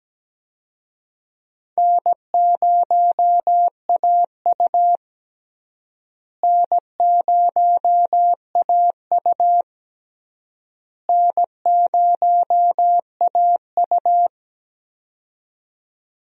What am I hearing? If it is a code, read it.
Morse code, heard as a keyed tone, about 17 wpm, 700 Hz. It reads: N0AU N0AU N0AU
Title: USA Call Signs - 4 Characters - Repeated 3x - 17wpm